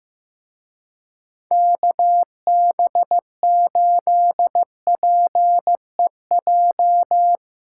Morse code, heard as a keyed tone, about 16 wpm, 700 Hz. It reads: KB8PEJ
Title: Speed Racing - USA 2x3 Call Signs 15wpm